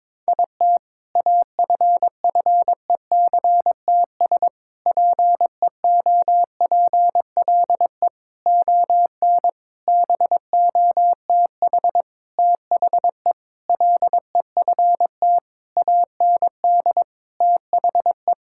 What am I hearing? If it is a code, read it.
Morse code, heard as a keyed tone, about 22 wpm, 700 Hz. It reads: IT AFFECTS PEOPLE ON BOTH THE LEFT AND THE
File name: Sentences with 7-Word Phrases - 22wpm